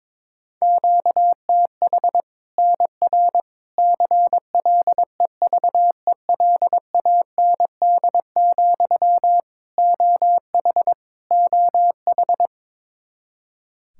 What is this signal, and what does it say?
Morse code, heard as a keyed tone, about 22 wpm, 700 Hz. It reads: QTH NR CLEVELAND, OH OH
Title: K1USN SST QSO Practice 22wpm